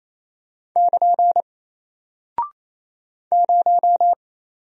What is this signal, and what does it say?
Morse code, heard as a keyed tone, about 28 wpm, 700 Hz. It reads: <KN> 0
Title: Single Character (Letter:Number:Punctuation:Prosign) Alt 28wpm